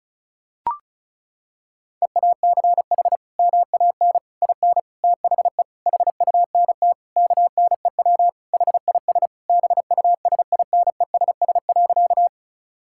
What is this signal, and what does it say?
Morse code, heard as a keyed tone, about 36 wpm, 700 Hz. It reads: EACH MAN IN THE HUNT KNEW HIS BUSINESS.
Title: Sentences from Top 1600 Words 35wpm